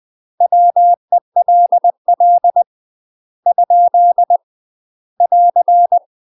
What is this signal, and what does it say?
Morse code, heard as a keyed tone, about 20 wpm, 700 Hz. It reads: WELL ? <AR>
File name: UK MASTER QSOs TRAINING / 20WPM / UK MASTER QSOs 0044